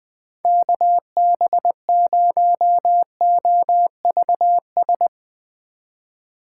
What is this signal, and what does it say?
Morse code, heard as a keyed tone, about 20 wpm, 700 Hz. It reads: KB0OVS
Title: Speed Racing - USA 2x3 Call Signs 17wpm